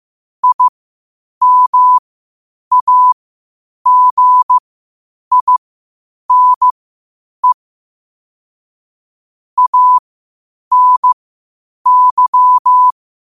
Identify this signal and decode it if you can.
Morse code, heard as a keyed tone, about 15 wpm, spaced as for 8 wpm, 1000 Hz. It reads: IMAGINE ANY
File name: qotdfiles / 8wpm